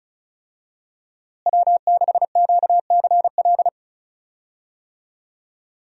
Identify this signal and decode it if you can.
Morse code, heard as a keyed tone, about 36 wpm, 700 Hz. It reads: W6QCL
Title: USA Call Signs - 5 Characters - Repeated 3x - 35wpm